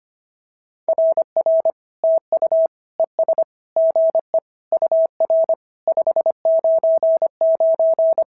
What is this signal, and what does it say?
Morse code, heard as a keyed tone, about 24 wpm, 650 Hz. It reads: RR TU ES GE UR 599
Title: SOTA QSO Practice 25wpm